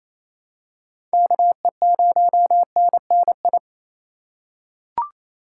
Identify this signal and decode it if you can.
Morse code, heard as a keyed tone, about 28 wpm, 700 Hz. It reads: KE0NNI
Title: USA Call Signs - 4x Character Spacing - 28wpm